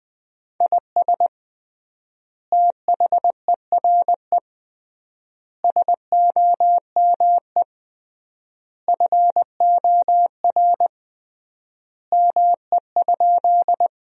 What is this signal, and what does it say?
Morse code, heard as a keyed tone, about 20 wpm, 700 Hz. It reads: IS THERE SOME FOR ME?
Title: 3x Word-Spacing - Sentences from Top 200 Words - 20wpm